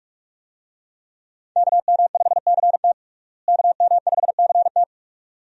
Morse code, heard as a keyed tone, about 46 wpm, 700 Hz. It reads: KM5CT KM5CT
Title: USA Call Signs - 5 Characters - Repeated 2x - 45wpm